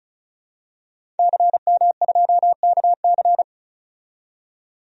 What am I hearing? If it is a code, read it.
Morse code, heard as a keyed tone, about 36 wpm, 700 Hz. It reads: CM2KC
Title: DXCC-100 Call Signs - 5 Characters - 35wpm